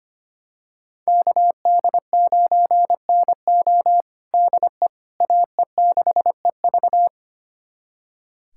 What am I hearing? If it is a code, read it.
Morse code, heard as a keyed tone, about 25 wpm, 700 Hz. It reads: KD9NO DE AE6EV
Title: USA Callsign DE Callsign - 5 Characters - 25wpm